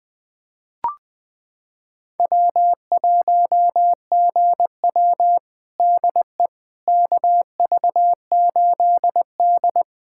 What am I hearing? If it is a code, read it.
Morse code, heard as a keyed tone, about 20 wpm, 700 Hz. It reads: W1GW DE KV8D
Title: USA Callsign DE Callsign - 4 Characters - 20wpm